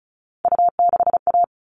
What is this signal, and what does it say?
Morse code, heard as a keyed tone, about 34 wpm, 700 Hz. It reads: U6A
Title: All Cut Numbers - 3-Digit 35wpm